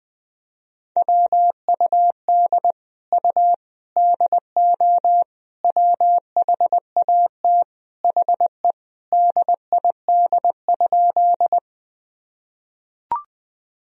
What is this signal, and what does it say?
Morse code, heard as a keyed tone, about 20 wpm, 700 Hz. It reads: WUD U DO WHAT HE DID?
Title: Sentences from Top 100 Words and Abbreviations 20wpm